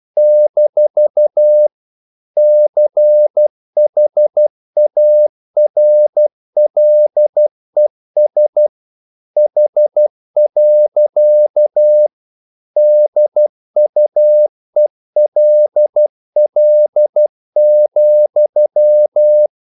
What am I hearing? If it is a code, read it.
Morse code, heard as a keyed tone, about 12 wpm, 600 Hz. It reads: - CHARLES H. DUELL,
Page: English quotations in CW (telegraphy - morse)